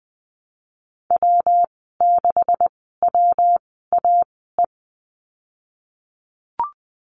Morse code, heard as a keyed tone, about 20 wpm, spaced as for 15 wpm, 700 Hz. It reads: W6WAE
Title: 173 Lesson 28g - Number 9 - Call Signs 20wpm